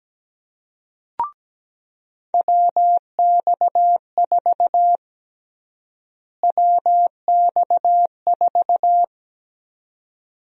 Morse code, heard as a keyed tone, about 17 wpm, 700 Hz. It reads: WX4 WX4
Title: USA Call Sign Prefixes - Repeated 2x - 17wpm